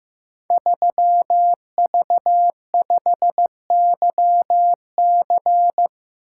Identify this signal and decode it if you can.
Morse code, heard as a keyed tone, about 15 wpm, 700 Hz. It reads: 3V5YC
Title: Code Groups - 5 Letter:Numbers - 15wpm